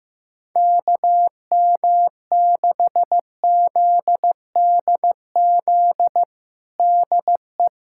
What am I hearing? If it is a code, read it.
Morse code, heard as a keyed tone, about 15 wpm, 700 Hz. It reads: KM6ZDZ DE
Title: USA Callsign DE Callsign 15wpm